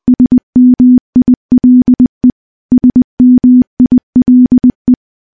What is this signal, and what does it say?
Morse code, heard as a keyed tone, about 20 wpm, 250 Hz. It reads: SMILE SMILE
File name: morse / morse-training-20WPM